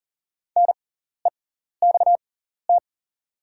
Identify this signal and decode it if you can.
Morse code, heard as a keyed tone, about 40 wpm, spaced as for 14 wpm, 690 Hz. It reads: NEXT
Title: 6x Character Spacing - Top 500 Words - 40wpm